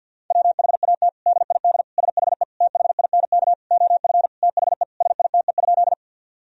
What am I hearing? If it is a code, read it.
Morse code, heard as a keyed tone, about 50 wpm, 700 Hz. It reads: WHAT DID SHE THINK OF THE SITE?
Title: Sentences from Top 1000 Words - Review Part 4 - 50wpm